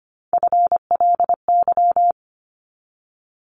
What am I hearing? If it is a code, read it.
Morse code, heard as a keyed tone, about 25 wpm, 700 Hz. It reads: FLY
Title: Top 500 Words 25wpm